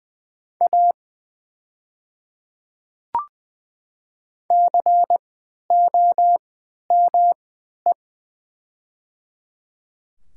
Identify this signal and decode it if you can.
Morse code, heard as a keyed tone, about 20 wpm, spaced as for 11 wpm, 700 Hz. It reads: A COME
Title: 3x Character Spacing - Top 200 Words - 20wpm